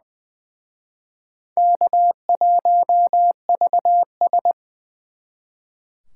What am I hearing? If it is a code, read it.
Morse code, heard as a keyed tone, about 23 wpm, 700 Hz. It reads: K1VS
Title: Speed Racing - USA 1x2 Call Signs 20wpm